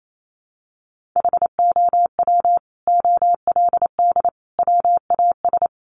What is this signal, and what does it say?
Morse code, heard as a keyed tone, about 28 wpm, 700 Hz. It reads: HOW OLD WAS
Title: Sentences from Top 200 Words 28wpm